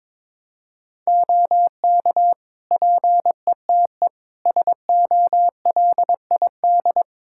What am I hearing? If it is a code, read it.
Morse code, heard as a keyed tone, about 22 wpm, 700 Hz. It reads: OK PETE SOLID
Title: Practice QSO - Exchange 3 - More Info - 22wpm